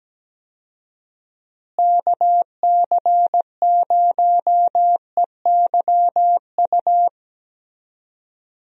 Morse code, heard as a keyed tone, about 17 wpm, 700 Hz. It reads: KC0EYU
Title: USA Call Signs - 6 Characters - Repeated 3x - 17wpm